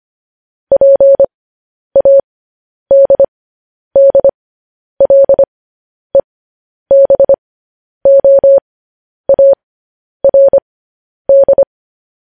Morse code, heard as a keyed tone, about 25 wpm, spaced as for 10 wpm, 550 Hz. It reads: PADDLEBOARD
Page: news headlines in morse code at 10 wpm